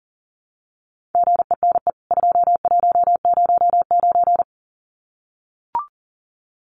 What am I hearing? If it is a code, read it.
Morse code, heard as a keyed tone, about 40 wpm, 700 Hz. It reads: GENE 2109
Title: CWT Exchange 40wpm